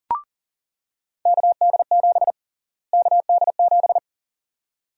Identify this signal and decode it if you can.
Morse code, heard as a keyed tone, about 40 wpm, 700 Hz. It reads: KD7 KD7
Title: USA Call Sign Prefixes - Repeated 2x - 40wpm